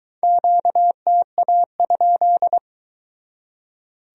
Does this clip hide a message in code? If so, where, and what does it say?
Yes: Morse code from 0.2 to 2.6 s, heard as a keyed tone, about 23 wpm, 700 Hz. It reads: QTA?